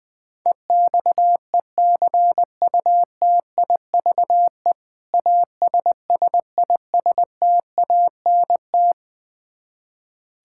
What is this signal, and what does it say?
Morse code, heard as a keyed tone, about 20 wpm, 700 Hz. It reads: EXECUTIVE ASSISTANT
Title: Practice QSO - Exchange 3 - More Info - 20wpm